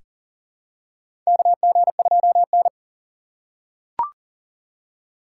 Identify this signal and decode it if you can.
Morse code, heard as a keyed tone, about 40 wpm, 700 Hz. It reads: KG2N
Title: USA Call Signs - 4 Characters - 4x Character Spacing - 40wpm